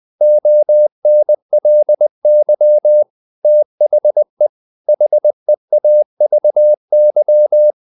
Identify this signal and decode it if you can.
Morse code, heard as a keyed tone, about 20 wpm, 600 Hz. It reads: ONLY THE HEAVY
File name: the old man and the sea part 2 20wpm